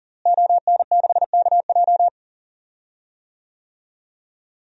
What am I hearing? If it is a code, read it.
Morse code, heard as a keyed tone, about 40 wpm, 700 Hz. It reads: ON6KJ
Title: DXCC-100 Call Signs - 5 Characters - Repeated 2x - 40wpm